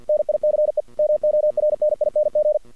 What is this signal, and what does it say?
Morse code, heard as a keyed tone, about 48 wpm, 620 Hz. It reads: NICE MORNING
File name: archived / CRSnet